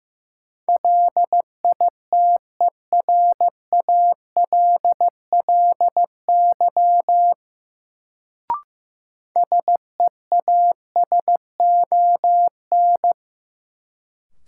Morse code, heard as a keyed tone, about 15 wpm, 700 Hz. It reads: LITERALLY SEASON
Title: Sentences from Top 1400 Words - New Word Drill - 15wpm